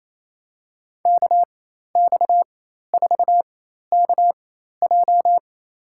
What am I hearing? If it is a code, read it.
Morse code, heard as a keyed tone, about 28 wpm, spaced as for 13 wpm, 700 Hz. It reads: KX4KJ
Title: USA Call Signs - 5 Characters - 4x Character Spacing - 28wpm